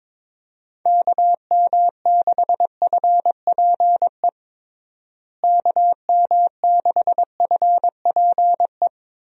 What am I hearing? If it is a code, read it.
Morse code, heard as a keyed tone, about 22 wpm, 700 Hz. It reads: KM6FPE KM6FPE
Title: USA Call Signs - 6 Characters - Repeated 2x - 22wpm